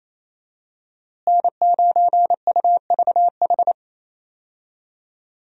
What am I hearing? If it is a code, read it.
Morse code, heard as a keyed tone, about 28 wpm, 700 Hz. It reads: N9UVH
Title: USA Call Signs - 5 Characters - Repeated 3x - 28wpm